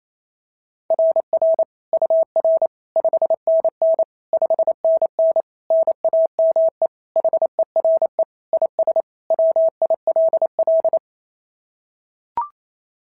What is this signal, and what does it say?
Morse code, heard as a keyed tone, about 25 wpm, 650 Hz. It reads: RR UR 5NN 5NN NAME HERE IS WILL
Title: SOTA QSO Practice 28wpm